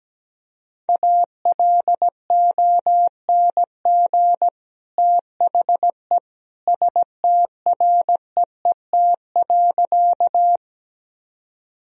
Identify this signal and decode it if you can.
Morse code, heard as a keyed tone, about 17 wpm, 700 Hz. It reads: ALONG THE STREET.
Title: Sentences from Top 1000 Words - Review Part 3 - 17wpm